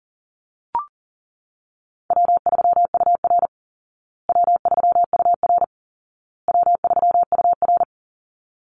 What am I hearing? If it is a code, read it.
Morse code, heard as a keyed tone, about 40 wpm, 700 Hz. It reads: W3UR W3UR W3UR
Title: Easy Call Signs Repeated 3x - 40wpm